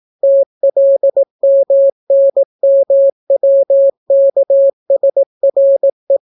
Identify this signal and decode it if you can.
Morse code, heard as a keyed tone, about 18 wpm, 550 Hz. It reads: TLMNMWKSRE